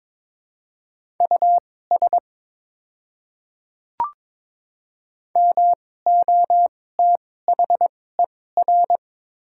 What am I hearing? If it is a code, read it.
Morse code, heard as a keyed tone, about 22 wpm, spaced as for 16 wpm, 700 Hz. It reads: US MOTHER
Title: 2x Character Spacing - Top 300 Words - 22wpm